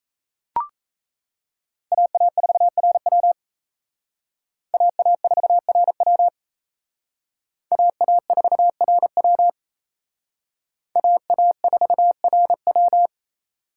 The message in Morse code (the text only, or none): AA4RW AA4RW AA4RW AA4RW